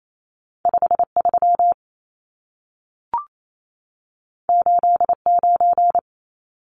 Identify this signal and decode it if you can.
Morse code, heard as a keyed tone, about 28 wpm, 700 Hz. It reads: 53 89
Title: Numbers - 2-Digit 28wpm